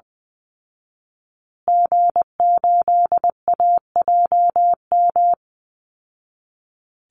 Morse code, heard as a keyed tone, about 20 wpm, 700 Hz. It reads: G8AJM
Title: Call Signs Repeated 2x - 20wpm